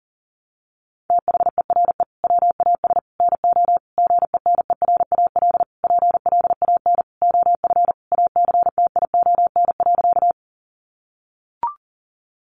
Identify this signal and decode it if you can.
Morse code, heard as a keyed tone, about 40 wpm, 700 Hz. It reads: THERE WAS NO GENERAL PLAN OF ACTION.
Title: Sentences from Top 700 Words 40wpm